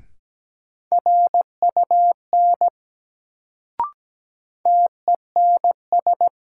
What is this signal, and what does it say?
Morse code, heard as a keyed tone, about 17 wpm, 700 Hz. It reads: RUN TENS
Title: Sets of 4 Words - New Word Drill - 17wpm